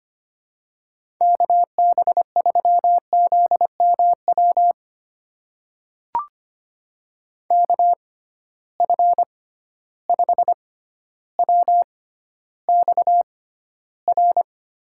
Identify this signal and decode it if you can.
Morse code, heard as a keyed tone, about 25 wpm, spaced as for 13 wpm, 700 Hz. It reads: KB3ZMW KF5WXR
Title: USA Call Signs - 6 Characters - 6x Character Spacing - 25wpm